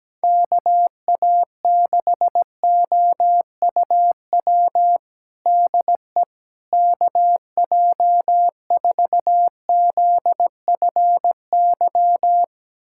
Morse code, heard as a keyed tone, about 17 wpm, 700 Hz. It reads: KA6OUW DE KJ4ZFY
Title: USA Callsign DE Callsign - 6 Characters - 17wpm